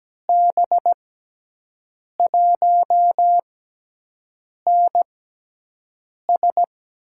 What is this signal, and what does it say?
Morse code, heard as a keyed tone, about 17 wpm, spaced as for 6 wpm, 700 Hz. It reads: B1NS
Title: USA Call Signs - 6 Characters - 6x Character Spacing - 17wpm